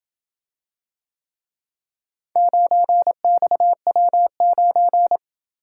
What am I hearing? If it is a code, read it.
Morse code, heard as a keyed tone, about 27 wpm, 700 Hz. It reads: 9XW9